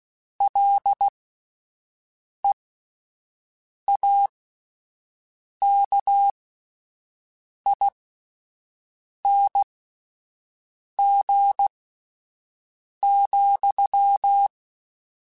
Morse code, heard as a keyed tone, about 16 wpm, spaced as for 5 wpm, 800 Hz. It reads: LEAKING,